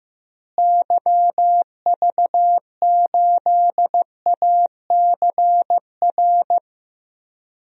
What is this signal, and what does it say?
Morse code, heard as a keyed tone, about 15 wpm, 700 Hz. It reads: YV8ACR
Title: DXCC-100 Call Signs - 6 Characters - Repeated 3x - 15wpm